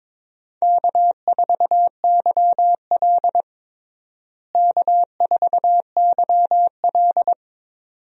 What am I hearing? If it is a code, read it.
Morse code, heard as a keyed tone, about 22 wpm, 700 Hz. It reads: K4YL K4YL
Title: USA Call Signs - 4 Characters - Repeated 2x - 22wpm